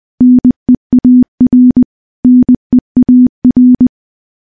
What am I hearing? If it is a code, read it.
Morse code, heard as a keyed tone, about 21 wpm, 250 Hz. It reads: NEAR NEAR
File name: morse-training-20WPM